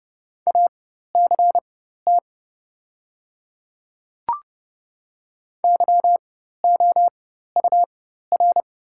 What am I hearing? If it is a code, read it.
Morse code, heard as a keyed tone, about 30 wpm, spaced as for 14 wpm, 700 Hz. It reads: ACT YOUR
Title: 4x Character Spacing - Top 200 Words - 30wpm